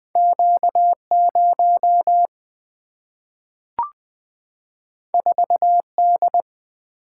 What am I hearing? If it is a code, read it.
Morse code, heard as a keyed tone, about 20 wpm, 700 Hz. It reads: Q0 4D
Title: Single-Letter Single-Number 20wpm